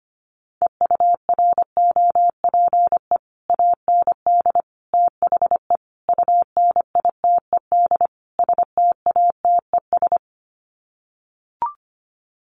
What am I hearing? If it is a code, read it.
Morse code, heard as a keyed tone, about 25 wpm, 700 Hz. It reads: EUROPE AND THE UNITED STATES
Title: Sets of 5 Words Encore2 25wpm